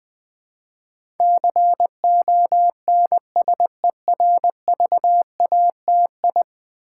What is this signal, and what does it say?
Morse code, heard as a keyed tone, about 20 wpm, 700 Hz. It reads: CONSERVATI
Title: Build A Word - 5 Letters At A Time - 12-Letter Words - 20wpm